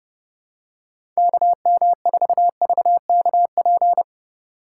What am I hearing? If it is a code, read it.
Morse code, heard as a keyed tone, about 29 wpm, 700 Hz. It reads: KM4VKP